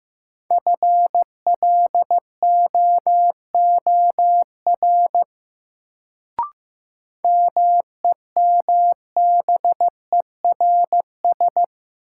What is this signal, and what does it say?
Morse code, heard as a keyed tone, about 15 wpm, 700 Hz. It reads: FLOOR MEMBERS
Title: Sets of 3 Words - New Word Drill - 15wpm